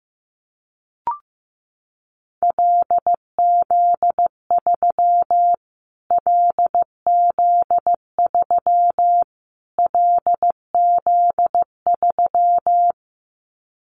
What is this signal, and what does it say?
Morse code, heard as a keyed tone, about 15 wpm, 700 Hz. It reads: LZ3 LZ3 LZ3
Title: Callsign Prefixes for Top 25 DXCC - Repeated 3x - 15wpm